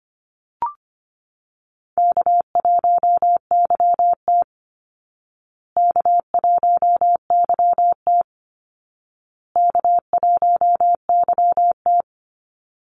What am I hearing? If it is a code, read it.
Morse code, heard as a keyed tone, about 25 wpm, 700 Hz. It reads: K1YT K1YT K1YT
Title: Easy Call Signs Repeated 3x - 25wpm